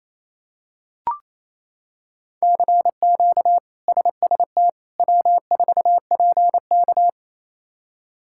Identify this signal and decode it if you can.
Morse code, heard as a keyed tone, about 28 wpm, 700 Hz. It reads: CQ SST W4PK
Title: K1USN SST QSO Practice - 28wpm